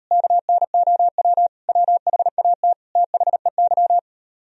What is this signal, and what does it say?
Morse code, heard as a keyed tone, about 38 wpm, 700 Hz. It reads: KNOW WHAT THEY